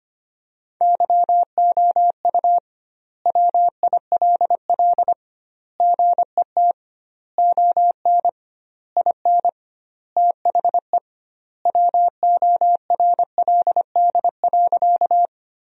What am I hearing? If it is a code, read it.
Morse code, heard as a keyed tone, about 25 wpm, 700 Hz. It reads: YOU WILL GET ON IN THE WORLD.
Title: 2x Word-Spacing - Sentences from Top 200 Words - 25wpm